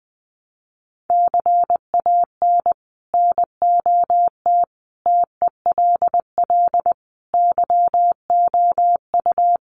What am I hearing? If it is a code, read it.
Morse code, heard as a keyed tone, about 20 wpm, 700 Hz. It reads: CAN NOT TELL YOU
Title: Sets of 5 Words Encore2 20wpm